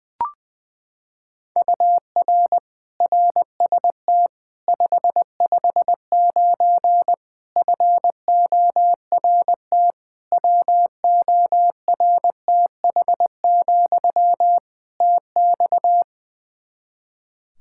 Morse code, heard as a keyed tone, about 20 wpm, 700 Hz. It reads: UR RST 559 FORT WORTH, TX
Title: Practice QSO - Exchange 1 - Beginning - 20wpm